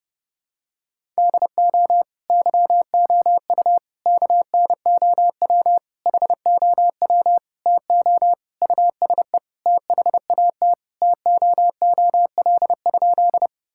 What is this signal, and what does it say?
Morse code, heard as a keyed tone, about 28 wpm, 700 Hz. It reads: DO YOU KNOW HOW TO USE THAT TOOL?
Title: Sentences from Top 1000 Words - Review Part 6 - 30wpm